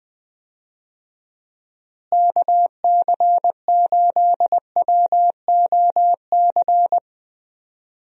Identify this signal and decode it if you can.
Morse code, heard as a keyed tone, about 20 wpm, 700 Hz. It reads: KC8WOC
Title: USA Call Signs - 6 Characters - Repeated 2x - 20wpm